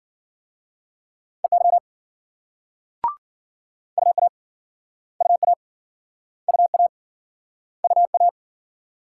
Morse code, heard as a keyed tone, about 53 wpm, 700 Hz. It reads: EY UA UA UA UA